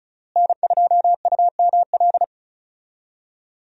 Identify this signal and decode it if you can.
Morse code, heard as a keyed tone, about 36 wpm, 700 Hz. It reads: N2UML
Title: USA Call Signs - 5 Characters - Repeated 3x - 35wpm